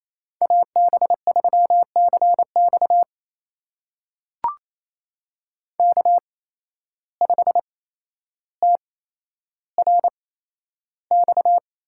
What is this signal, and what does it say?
Morse code, heard as a keyed tone, about 28 wpm, spaced as for 12 wpm, 700 Hz. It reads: AB3CX K5TRX
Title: USA Call Signs - 8x Character Spacing - 28wpm